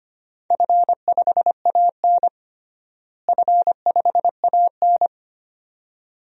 Morse code, heard as a keyed tone, about 25 wpm, 700 Hz. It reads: F5AN F5AN
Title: DXCC-100 Call Signs - 4 Characters - Repeated 2x - 25wpm